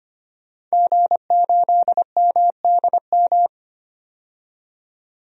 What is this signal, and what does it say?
Morse code, heard as a keyed tone, about 25 wpm, 700 Hz. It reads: G8MDM
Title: DXCC-100 Call Signs - 5 Characters - 25wpm